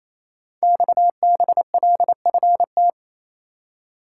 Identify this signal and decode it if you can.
Morse code, heard as a keyed tone, about 28 wpm, 700 Hz. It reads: XBLFT
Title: Code Groups - 5 Letters - 28wpm